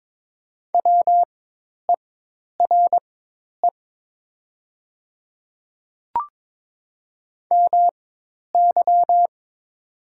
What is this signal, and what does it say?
Morse code, heard as a keyed tone, about 22 wpm, spaced as for 10 wpm, 700 Hz. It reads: WERE MY
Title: 4x Character Spacing - Top 100 Words - 22wpm